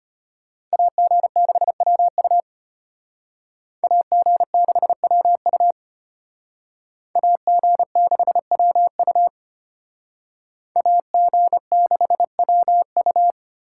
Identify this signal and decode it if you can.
Morse code, heard as a keyed tone, about 32 wpm, 700 Hz. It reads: AG6WU AG6WU AG6WU AG6WU